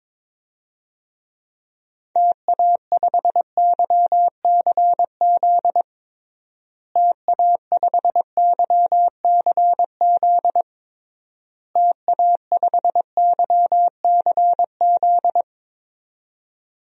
Morse code, heard as a keyed tone, about 23 wpm, 700 Hz. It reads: TA5YCZ TA5YCZ TA5YCZ
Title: DXCC-100 Call Signs - 6 Characters - Repeated 3x - 22wpm